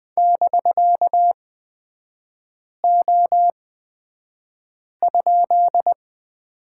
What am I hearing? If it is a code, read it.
Morse code, heard as a keyed tone, about 20 wpm, spaced as for 5 wpm, 700 Hz. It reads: <BK>O?
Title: 250 Lesson 40e - BK - Three Characters 20wpm